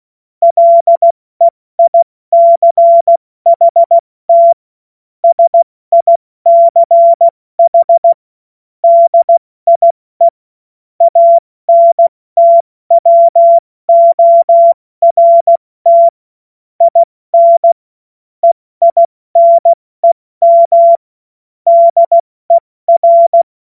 Morse code, heard as a keyed tone, about 16 wpm, 680 Hz. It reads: LEICHT SICH DIE ANTWORT IN EINEM DER
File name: Morsekurs / Witze / Witz 09 Januar 2024 70 F80